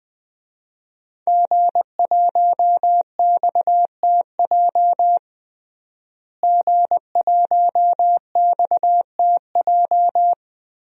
Morse code, heard as a keyed tone, about 21 wpm, 700 Hz. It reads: G1XTJ G1XTJ
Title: DXCC-100 Call Signs - 5 Characters - Repeated 3x - 20wpm